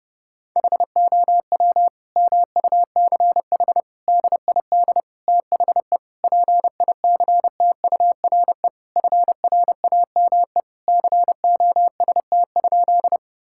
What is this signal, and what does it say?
Morse code, heard as a keyed tone, about 30 wpm, 700 Hz. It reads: HOW MUCH DID THE PICTURE FRAME COST?
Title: Sentences from Top 1000 Words - Review Part 6 - 30wpm